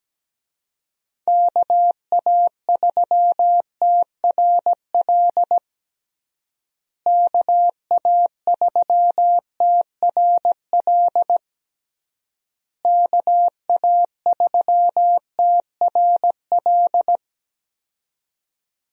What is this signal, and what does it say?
Morse code, heard as a keyed tone, about 17 wpm, 700 Hz. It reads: KA3TRL KA3TRL KA3TRL
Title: USA Call Signs - 6 Characters - Repeated 3x - 17wpm